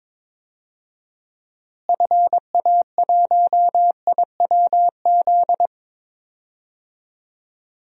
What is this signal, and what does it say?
Morse code, heard as a keyed tone, about 21 wpm, 700 Hz. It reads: FA1IWZ
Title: DXCC-100 Call Signs - 6 Characters - Repeated 3x - 22wpm